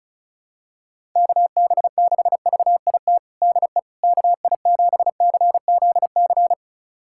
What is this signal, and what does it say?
Morse code, heard as a keyed tone, about 34 wpm, 700 Hz. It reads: KB6VIT DE KI7CZC
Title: USA Callsign DE Callsign - 6 Characters - 35wpm